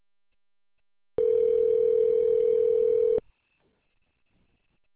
Ringback tone (North America), 440 + 480 Hz.